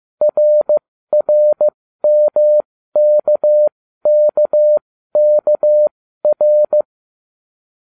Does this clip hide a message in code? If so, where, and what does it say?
Yes: Morse code from 0.2 to 6.8 s, heard as a keyed tone, about 15 wpm, spaced as for 13 wpm, 600 Hz. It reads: RRMKKKR